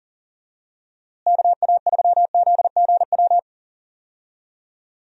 Morse code, heard as a keyed tone, about 38 wpm, 700 Hz. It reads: KA3ZGW
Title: USA Call Signs - 6 Characters - Repeated 3x - 40wpm